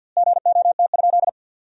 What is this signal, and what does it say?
Morse code, heard as a keyed tone, about 50 wpm, 700 Hz. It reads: GOT?
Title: 2x Word-Spacing - Sentences from Top 300 Words - 50wpm